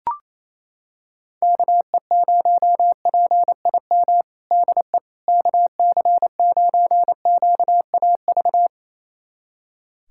Morse code, heard as a keyed tone, about 28 wpm, 700 Hz. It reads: KE0PIM DE KC9QAV
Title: USA Callsign DE Callsign - 6 Characters - 28wpm